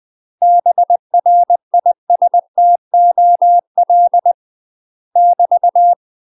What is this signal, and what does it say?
Morse code, heard as a keyed tone, about 20 wpm, 700 Hz. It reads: BRISTOL <BT>
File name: UK MASTER QSOs TRAINING / 20WPM / UK MASTER QSOs 0057